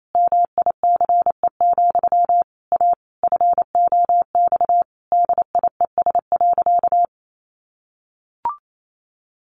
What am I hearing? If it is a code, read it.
Morse code, heard as a keyed tone, about 28 wpm, 700 Hz. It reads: MICE, A FOX DIES.
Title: A Sound Of Thunder 28wpm